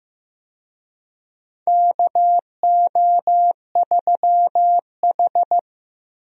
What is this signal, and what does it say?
Morse code, heard as a keyed tone, about 15 wpm, 700 Hz. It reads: KO3H
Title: USA Call Signs - 4 Characters - Repeated 3x - 15wpm